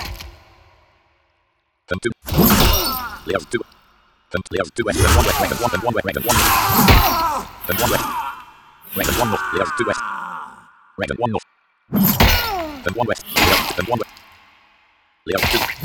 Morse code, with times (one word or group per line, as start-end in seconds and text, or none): none